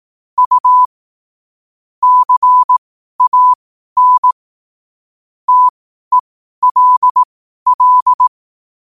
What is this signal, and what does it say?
Morse code, heard as a keyed tone, about 18 wpm, spaced as for 12 wpm, 1000 Hz. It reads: U CAN TELL
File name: qotdfiles / 12wpm